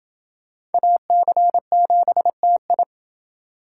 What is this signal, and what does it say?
Morse code, heard as a keyed tone, about 28 wpm, 700 Hz. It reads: AC7TI